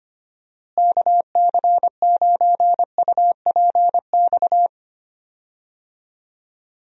Morse code, heard as a keyed tone, about 25 wpm, 700 Hz. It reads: KC9UPX
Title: USA Call Signs - 6 Characters - Repeated 2x - 25wpm